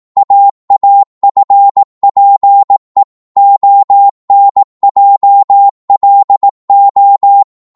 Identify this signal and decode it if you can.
Morse code, heard as a keyed tone, about 18 wpm, 800 Hz. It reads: AAFPE ONJLO
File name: Morse / ltr-18wpm